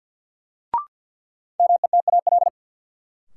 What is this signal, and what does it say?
Morse code, heard as a keyed tone, about 49 wpm, 700 Hz. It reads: METAL
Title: Sentences from Top 1000 Words - Review Part 6 - New Word Drill - 50wpm